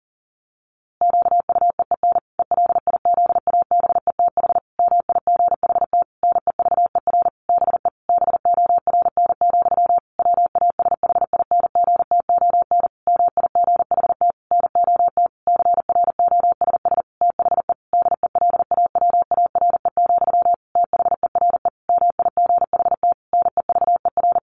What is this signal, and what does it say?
Morse code, heard as a keyed tone, about 40 wpm, 700 Hz. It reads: QUEEN ELIZABETH MIGHT NEVER BE BORN, WASHINGTON MIGHT NOT CROSS THE DELAWARE, THERE MIGHT NEVER
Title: A Sound Of Thunder 40wpm